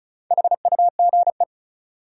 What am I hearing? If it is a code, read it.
Morse code, heard as a keyed tone, about 36 wpm, 700 Hz. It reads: HUGE
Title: Sentences from Top 1100 Words - New Word Drill Alt - 35wpm